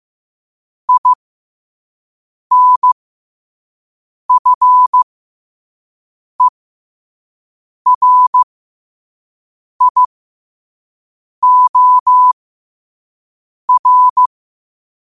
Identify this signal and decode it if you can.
Morse code, heard as a keyed tone, about 15 wpm, spaced as for 5 wpm, 1000 Hz. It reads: INFERIOR